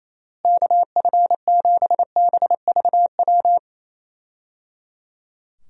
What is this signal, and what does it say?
Morse code, heard as a keyed tone, about 28 wpm, 700 Hz. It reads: KF7BVW
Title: USA Call Signs - 6 Characters - Repeated 3x - 28wpm